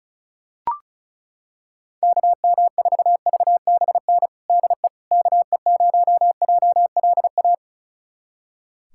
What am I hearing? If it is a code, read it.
Morse code, heard as a keyed tone, about 36 wpm, 700 Hz. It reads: KM4VBN DE KE0JLA
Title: USA Callsign DE Callsign - 6 Characters - 35wpm